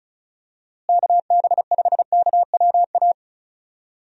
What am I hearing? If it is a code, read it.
Morse code, heard as a keyed tone, about 34 wpm, 700 Hz. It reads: KB5KWA
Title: USA Call Signs - 6 Characters - Repeated 3x - 35wpm